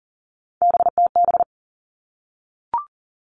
Morse code, heard as a keyed tone, about 40 wpm, 700 Hz. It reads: BTB
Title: All Cut Numbers - 3-Digit 40wpm